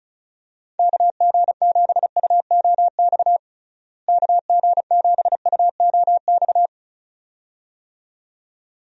Morse code, heard as a keyed tone, about 36 wpm, 700 Hz. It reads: KG7UOX KG7UOX
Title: USA Call Signs - 6 Characters - Repeated 2x - 35wpm